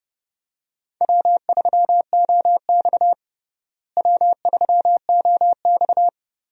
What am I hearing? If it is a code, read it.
Morse code, heard as a keyed tone, about 32 wpm, 700 Hz. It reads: W3OX W3OX
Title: USA Call Signs - 4 Characters - Repeated 2x - 30wpm